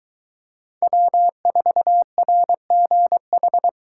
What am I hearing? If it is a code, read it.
Morse code, heard as a keyed tone, about 23 wpm, 700 Hz. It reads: W4RGH